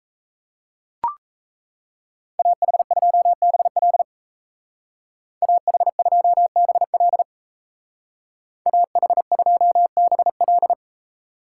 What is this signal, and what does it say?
Morse code, heard as a keyed tone, about 37 wpm, 700 Hz. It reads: AH2BL AH2BL AH2BL